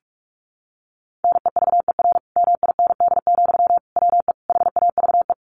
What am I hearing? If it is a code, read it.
Morse code, heard as a keyed tone, about 44 wpm, 700 Hz. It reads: NEVER MIND, WE HAVE
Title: Sentences from Top 1300 Words 45wpm